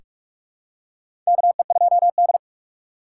Morse code, heard as a keyed tone, about 46 wpm, 700 Hz. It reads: KE2D
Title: K1USN SST QSO Practice - 45wpm